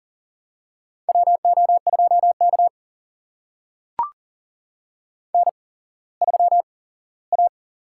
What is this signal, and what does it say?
Morse code, heard as a keyed tone, about 40 wpm, spaced as for 19 wpm, 700 Hz. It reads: WO2K N3A
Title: USA Call Signs - 4 Characters - 8x Character Spacing - 40wpm